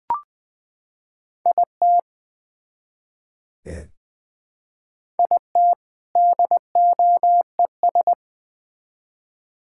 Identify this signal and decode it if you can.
Morse code, heard as a keyed tone, about 20 wpm, 700 Hz. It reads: IT IT DOES
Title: Build A Phrase - 2-Word Phrases - 20wpm